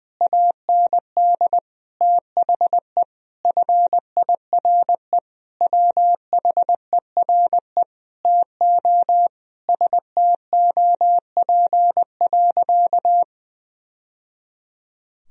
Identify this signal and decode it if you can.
Morse code, heard as a keyed tone, about 20 wpm, 700 Hz. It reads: AND THE FIRE WHERE TO STOP.
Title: Sentences from Top 1000 Words - Review Part 2 - 20wpm